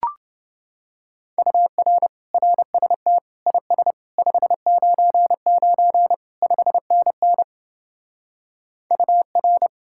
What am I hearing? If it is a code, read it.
Morse code, heard as a keyed tone, about 28 wpm, 700 Hz. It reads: UR RST IS 599 5NN UR